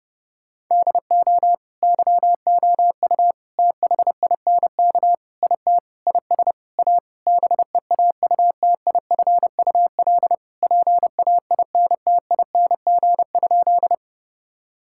Morse code, heard as a keyed tone, about 30 wpm, 700 Hz. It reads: DO YOU THINK IT IS A BEAUTIFUL PAINTING?
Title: Sentences from Top 1000 Words - Review Part 6 - 30wpm